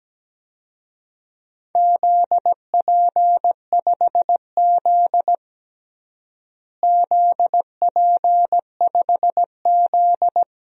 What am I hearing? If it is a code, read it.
Morse code, heard as a keyed tone, about 17 wpm, 700 Hz. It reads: ZP5Z ZP5Z
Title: DXCC-100 Call Signs - 4 Characters - Repeated 2x - 17wpm